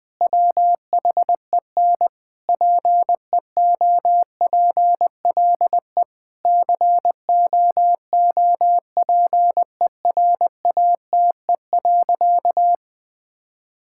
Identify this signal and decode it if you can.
Morse code, heard as a keyed tone, about 19 wpm, 700 Hz. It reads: WHEN PEOPLE COOPERATE.
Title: Difficult Sentences with Hints - 20wpm